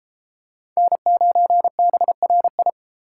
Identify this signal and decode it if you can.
Morse code, heard as a keyed tone, about 30 wpm, 700 Hz. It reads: N9BRI